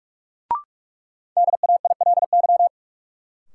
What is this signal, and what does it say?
Morse code, heard as a keyed tone, about 46 wpm, 700 Hz. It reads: DAILY
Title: Sentences from Top 1100 Words - New Word Drill Alt - 45wpm